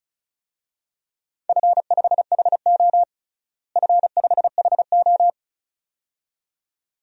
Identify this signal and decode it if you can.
Morse code, heard as a keyed tone, about 36 wpm, 700 Hz. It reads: F5HO F5HO
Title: DXCC-100 Call Signs - 4 Characters - Repeated 2x - 35wpm